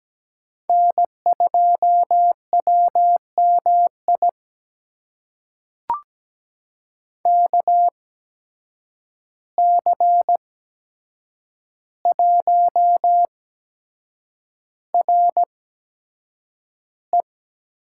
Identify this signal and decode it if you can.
Morse code, heard as a keyed tone, about 17 wpm, spaced as for 7 wpm, 700 Hz. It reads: N2WMI KC1RE
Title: USA Call Signs - 5 Characters - 8x Character Spacing - 17wpm